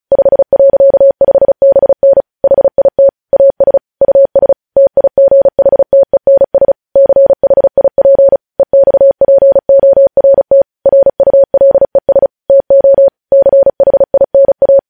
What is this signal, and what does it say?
Morse code, heard as a keyed tone, about 36 wpm, 550 Hz. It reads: 5.5BN HIT AS US TIGHTENS CHIP EXPORT RULES TO CHINA
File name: podcast / News headlines-1x-35x35